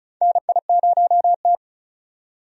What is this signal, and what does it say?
Morse code, heard as a keyed tone, about 36 wpm, 700 Hz. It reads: NI0T
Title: USA Call Signs - 4 Characters - 35wpm